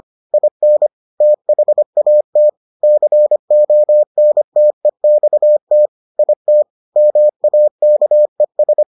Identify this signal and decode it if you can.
Morse code, heard as a keyed tone, about 25 wpm, 600 Hz. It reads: IN THAT CONTEXT IT MAKES